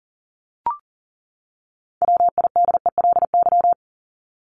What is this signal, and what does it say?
Morse code, heard as a keyed tone, about 40 wpm, 700 Hz. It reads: WIDELY
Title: Top 1001-1250 Words 40wpm